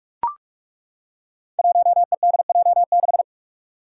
Morse code, heard as a keyed tone, about 41 wpm, 700 Hz. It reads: 1EDJ6
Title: Five Challenging Letter-Number Combinations 45wpm